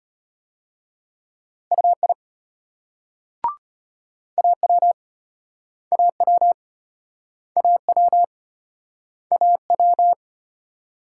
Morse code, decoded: UI AW AW AW AW

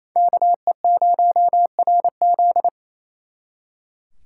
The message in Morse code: KE0RZ

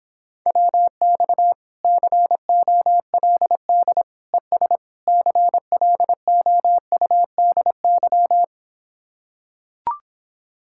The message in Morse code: WX COLD ES CLOUDY

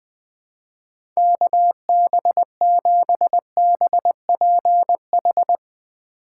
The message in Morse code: KB7BPH

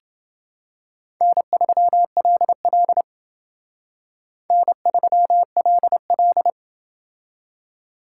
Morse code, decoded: N3LL N3LL

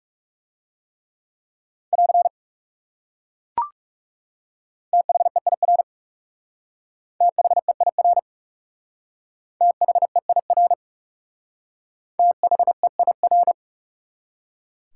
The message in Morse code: <AR> THEIR THEIR THEIR THEIR

